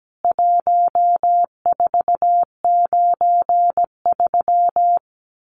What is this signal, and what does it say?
Morse code, heard as a keyed tone, about 17 wpm, 700 Hz. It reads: 1493
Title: CWT Exchange 17wpm